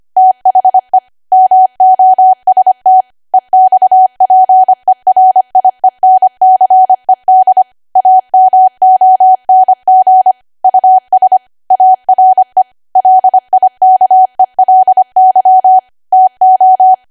Morse code, heard as a keyed tone, about 25 wpm, 750 Hz. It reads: THE MOST EXPERIENCED AMONG US ARE LIKELY TO
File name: Archive / 25 WPM / 25WPM